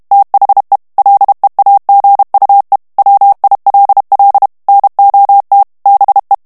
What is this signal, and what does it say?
Morse code, heard as a keyed tone, about 32 wpm, 800 Hz. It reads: THE LEAGUE WILL NOT BE
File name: ANGLAIS texte clair 3 vitesse 30 mots minute